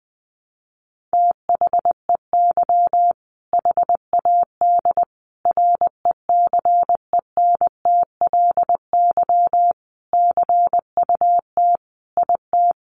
Morse code, heard as a keyed tone, about 20 wpm, 700 Hz. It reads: THEY HAD RECENTLY CUT IT
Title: Sentences from Top 1200 Words 20wpm